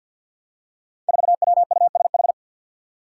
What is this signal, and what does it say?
Morse code, heard as a keyed tone, about 53 wpm, 700 Hz. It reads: 4WUSH
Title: Five Challenging Letter-Number Combinations 50wpm